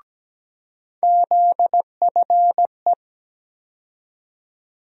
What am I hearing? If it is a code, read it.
Morse code, heard as a keyed tone, about 17 wpm, 700 Hz. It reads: ZFE
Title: USA Call Sign Postfixes - 3 Letters 17wpm